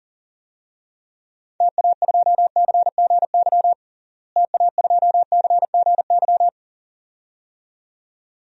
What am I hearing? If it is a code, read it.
Morse code, heard as a keyed tone, about 40 wpm, 700 Hz. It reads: TA2CGY TA2CGY
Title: DXCC-100 Call Signs - 6 Characters - Repeated 2x - 40wpm